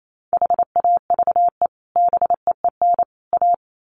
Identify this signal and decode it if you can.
Morse code, heard as a keyed tone, about 28 wpm, 700 Hz. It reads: HAVE BEEN A
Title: Sets of 3 Words 28wpm